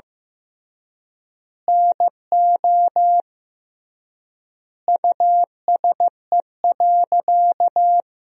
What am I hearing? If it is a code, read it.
Morse code, heard as a keyed tone, about 15 wpm, 700 Hz. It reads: NO USE.
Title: 3x Word-Spacing - Sentences from Top 100 Words - 15wpm